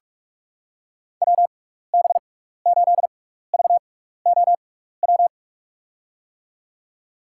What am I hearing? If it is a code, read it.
Morse code, heard as a keyed tone, about 46 wpm, spaced as for 15 wpm, 700 Hz. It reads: WB8VOW
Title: USA Call Signs - 6x Character Spacing - 45wpm